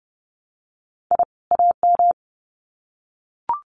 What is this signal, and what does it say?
Morse code, heard as a keyed tone, about 30 wpm, 700 Hz. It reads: I AM
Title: Sets of 2 Words 30wpm